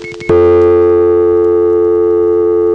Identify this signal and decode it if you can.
Dial tone (North America), 350 + 440 Hz.